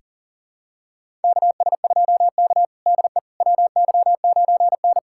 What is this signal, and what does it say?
Morse code, heard as a keyed tone, about 40 wpm, 700 Hz. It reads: KS2K DE WY9N
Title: USA Callsign DE Callsign - 4 Characters - 40wpm